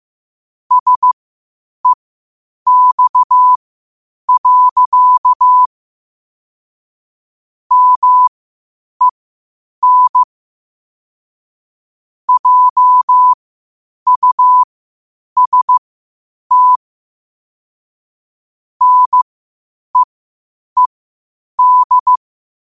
Morse code, heard as a keyed tone, about 15 wpm, spaced as for 8 wpm, 1000 Hz. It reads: SEX. MEN JUST NEED
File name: qotdfiles / 8wpm